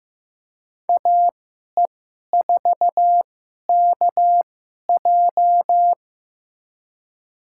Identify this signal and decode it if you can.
Morse code, heard as a keyed tone, about 15 wpm, spaced as for 11 wpm, 700 Hz. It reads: AE4KJ
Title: USA Call Signs - 2x Character Spacing - 15wpm